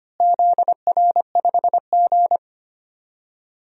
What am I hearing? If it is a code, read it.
Morse code, heard as a keyed tone, about 25 wpm, 700 Hz. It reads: ZR5G
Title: DXCC-100 Call Signs - 4 Characters - Repeated 3x - 25wpm